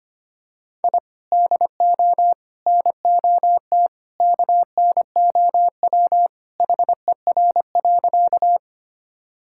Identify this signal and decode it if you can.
Morse code, heard as a keyed tone, about 25 wpm, 700 Hz. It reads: I DO NOT KNOW HER.
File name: practice / 25wpm / Sentences with 2-Word Phrases - 25wpm